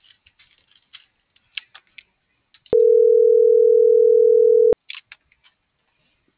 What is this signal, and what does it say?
Ringback tone (North America), 440 + 480 Hz.